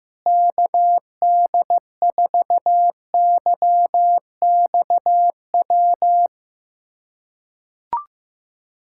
Morse code, heard as a keyed tone, about 15 wpm, 700 Hz. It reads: KD4YXW
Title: USA Callsign DE Callsign 15wpm